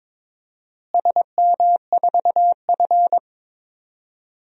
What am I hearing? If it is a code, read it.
Morse code, heard as a keyed tone, about 22 wpm, 700 Hz. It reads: SM4F
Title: DXCC-100 Call Signs - 4 Characters - Repeated 2x - 22wpm